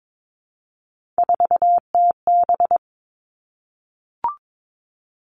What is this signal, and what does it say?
Morse code, heard as a keyed tone, about 22 wpm, 700 Hz. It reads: 4TB
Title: All Cut Numbers - 3-Digit 22wpm